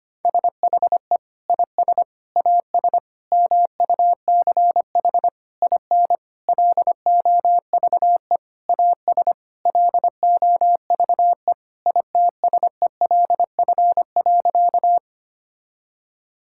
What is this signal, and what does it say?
Morse code, heard as a keyed tone, about 25 wpm, 700 Hz. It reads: SHE IS AS MUCH IN LOVE AS LOVE ITSELF.
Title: Sentences from Top 1000 Words - Review Part 5 - 25wpm